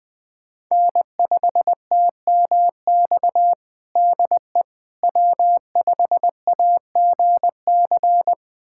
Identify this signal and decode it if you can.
Morse code, heard as a keyed tone, about 21 wpm, 700 Hz. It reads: N5TMX DE W5AGC
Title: USA Callsign DE Callsign - 5 Characters - 20wpm